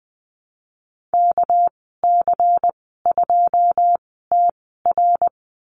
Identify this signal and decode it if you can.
Morse code, heard as a keyed tone, about 20 wpm, spaced as for 15 wpm, 700 Hz. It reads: KC2TR
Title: 213 Lesson 34g - Number 8 - Call Signs 20wpm